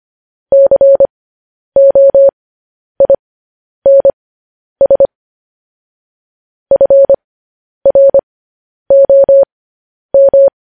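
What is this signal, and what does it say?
Morse code, heard as a keyed tone, about 25 wpm, spaced as for 10 wpm, 550 Hz. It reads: COINS FROM
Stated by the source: Morse code news headlines sent at 10 words per minute.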